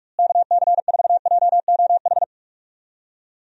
Morse code, heard as a keyed tone, about 46 wpm, 700 Hz. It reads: KC4JOH
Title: USA Call Signs - 6 Characters - Repeated 2x - 45wpm